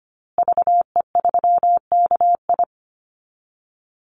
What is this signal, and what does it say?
Morse code, heard as a keyed tone, about 25 wpm, 700 Hz. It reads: VE3KI
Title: Call Signs Repeated 2x - 25wpm